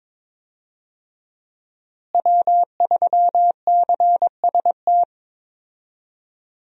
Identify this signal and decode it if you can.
Morse code, heard as a keyed tone, about 22 wpm, 700 Hz. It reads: W3CST
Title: USA Call Signs - 5 Characters - Repeated 2x - 22wpm